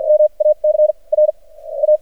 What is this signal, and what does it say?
Morse code, heard as a keyed tone, about 53 wpm, spaced as for 39 wpm, 610 Hz. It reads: MAC R A
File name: archived / CRSnet